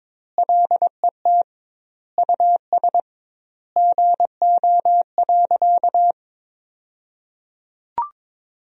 Morse code, heard as a keyed tone, about 22 wpm, 700 Hz. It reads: LET US GO.
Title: 2x Word-Spacing - Sentences from Top 400 Words - 22wpm